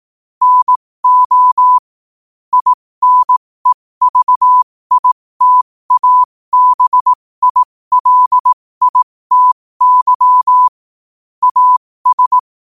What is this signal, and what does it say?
Morse code, heard as a keyed tone, about 18 wpm, spaced as for 15 wpm, 1000 Hz. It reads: NO INEVITABILITY AS
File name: qotdfiles / 15wpm